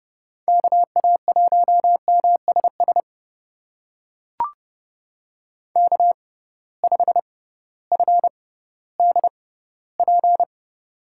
Morse code, heard as a keyed tone, about 30 wpm, spaced as for 16 wpm, 700 Hz. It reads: KA1MSS K5FDP
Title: USA Call Signs - 6x Character Spacing - 30wpm